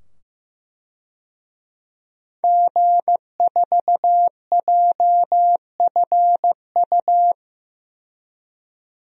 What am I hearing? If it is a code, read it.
Morse code, heard as a keyed tone, about 15 wpm, 700 Hz. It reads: G4JFU
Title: DXCC-100 Call Signs - 5 Characters - 15wpm